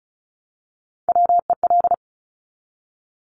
Morse code, heard as a keyed tone, about 36 wpm, 700 Hz. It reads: WEL
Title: Top Three Letter Combinations in Words 35wpm